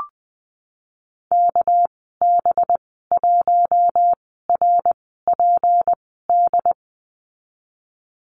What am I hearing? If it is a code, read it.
Morse code, heard as a keyed tone, about 20 wpm, spaced as for 15 wpm, 700 Hz. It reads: KB1RPD
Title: 207 Lesson 33g - Letter j - Call Signs 20wpm